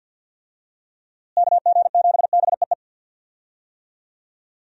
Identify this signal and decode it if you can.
Morse code, heard as a keyed tone, about 51 wpm, 700 Hz. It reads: KG7BEE
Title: USA Call Signs - 6 Characters - Repeated 3x - 50wpm